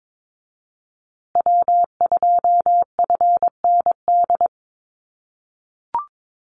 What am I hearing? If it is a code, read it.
Morse code, heard as a keyed tone, about 22 wpm, 700 Hz. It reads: W2FND
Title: USA Call Signs Repeated 2x 22wpm